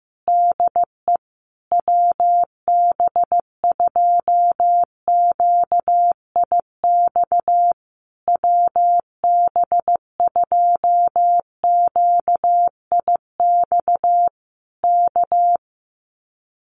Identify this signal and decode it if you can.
Morse code, heard as a keyed tone, about 15 wpm, 700 Hz. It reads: DE WB2QIX WB2QIX K
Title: Common QSO Phrases Encore 15wpm